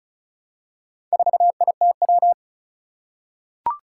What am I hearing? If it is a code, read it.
Morse code, heard as a keyed tone, about 36 wpm, 700 Hz. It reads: 4ITW
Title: Four Challenging Letter-Number Combinations 35wpm